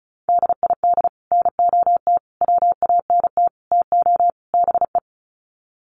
Morse code, heard as a keyed tone, about 34 wpm, 700 Hz. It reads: DID NOT WANT TO BE
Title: Sets of 5 Words Encore 35wpm